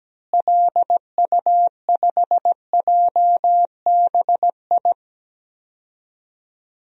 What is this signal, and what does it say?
Morse code, heard as a keyed tone, about 17 wpm, 700 Hz. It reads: LU5JBI
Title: DXCC-100 Call Signs - 6 Characters - Repeated 3x - 17wpm